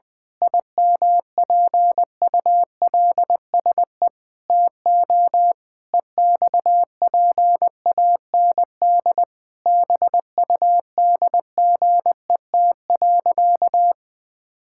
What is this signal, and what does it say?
Morse code, heard as a keyed tone, about 20 wpm, 700 Hz. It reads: IMPULSE TO EXPAND BUDGET.
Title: Pangram Challenge v2 - 20wpm